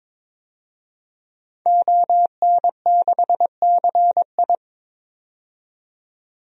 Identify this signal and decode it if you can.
Morse code, heard as a keyed tone, about 22 wpm, 700 Hz. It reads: ON6CI
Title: DXCC-100 Call Signs - 5 Characters - 22wpm